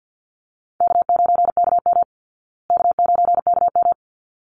Morse code, heard as a keyed tone, about 49 wpm, 700 Hz. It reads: K9KM K9KM
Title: Easy Call Signs Repeated 2x - 50wpm